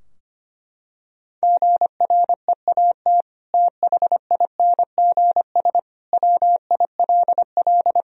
Morse code, heard as a keyed tone, about 25 wpm, 700 Hz. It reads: GREAT THINGS WILL
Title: Sentences from Top 1000 Words - Review Part 4 - 25wpm